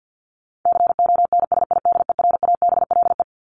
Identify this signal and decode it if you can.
Morse code, heard as a keyed tone, about 48 wpm, 700 Hz. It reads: CONSIDERABLE
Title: LY Suffix - Top 100 Words with Base Words 50wpm